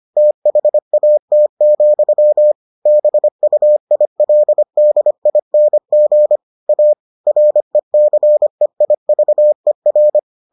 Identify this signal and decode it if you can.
Morse code, heard as a keyed tone, about 25 wpm, 600 Hz. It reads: THAT, BUILDING A RECEIVER